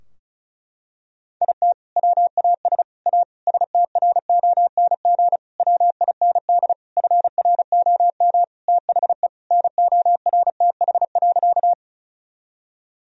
Morse code, heard as a keyed tone, about 36 wpm, 700 Hz. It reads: IT WAS A STRONG WIND FROM THE NORTH.
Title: Sentences from Top 1000 Words - Review Part 2 - 35wpm